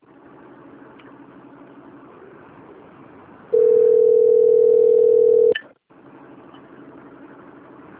Ringback tone (North America), 440 + 480 Hz.
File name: rg-1600-unknown- - - .85167